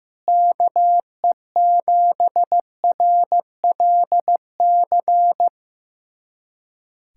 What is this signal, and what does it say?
Morse code, heard as a keyed tone, about 15 wpm, 700 Hz. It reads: KE7RLC
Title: USA Callsign DE Callsign 15wpm